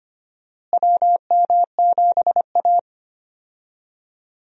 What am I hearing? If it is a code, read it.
Morse code, heard as a keyed tone, about 25 wpm, 700 Hz. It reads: WM7A
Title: USA Call Signs - 4 Characters - Repeated 3x - 25wpm